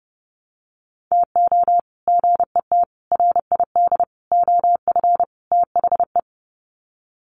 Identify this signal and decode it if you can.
Morse code, heard as a keyed tone, about 30 wpm, 700 Hz. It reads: TO GET RID OF THE
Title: Sets of 5 Words 30wpm